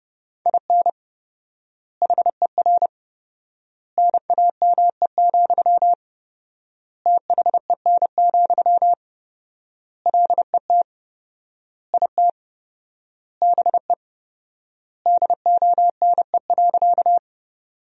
Morse code, heard as a keyed tone, about 30 wpm, 700 Hz. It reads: IN HER NAME, THEN, LET IT BE DONE.
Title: 4x Word-Spacing - Sentences from Top 500 Words - 30wpm